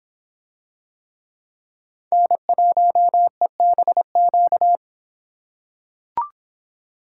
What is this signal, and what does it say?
Morse code, heard as a keyed tone, about 26 wpm, 700 Hz. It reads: N1EBQ